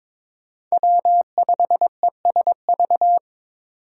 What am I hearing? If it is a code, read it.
Morse code, heard as a keyed tone, about 26 wpm, 700 Hz. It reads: W5ESV
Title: Speed Racing - USA Call Signs 22wpm